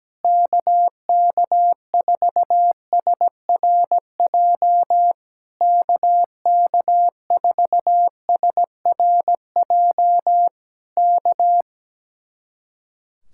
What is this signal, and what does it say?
Morse code, heard as a keyed tone, about 17 wpm, 700 Hz. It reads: KK4SRJ KK4SRJ K
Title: Speed-Racing Common QSO Phrases - 17wpm